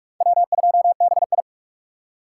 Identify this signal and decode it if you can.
Morse code, heard as a keyed tone, about 59 wpm, spaced as for 51 wpm, 700 Hz. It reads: W2BI
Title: Speed Racing - USA 1x2 Call Signs 45wpm